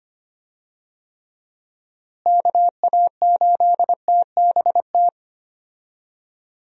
Morse code, heard as a keyed tone, about 25 wpm, 700 Hz. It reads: KA8TBT
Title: USA Call Signs - 6 Characters - Repeated 2x - 25wpm